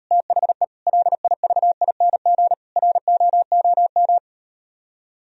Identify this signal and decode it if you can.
Morse code, heard as a keyed tone, about 38 wpm, 700 Hz. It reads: THE LIVING ROOM